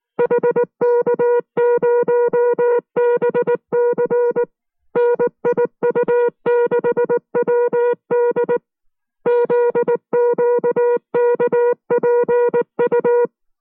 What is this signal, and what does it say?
Morse code, heard as a keyed tone, about 19 wpm, 480 Hz. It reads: HK0BC NIU6WD ZQKPU